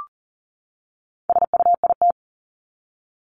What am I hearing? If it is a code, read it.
Morse code, heard as a keyed tone, about 40 wpm, 700 Hz. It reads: SUIT
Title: Top 1501-1750 Words 40wpm